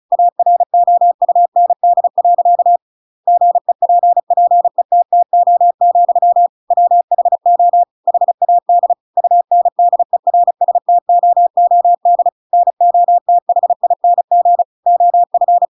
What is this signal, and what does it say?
Morse code, heard as a keyed tone, about 34 wpm, 700 Hz. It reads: AROUND. GEPPETTO, WHO HAD UNDERSTOOD NOTHING OF
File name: Pinocchio / Pinocchio ch7 35 35 ik1hge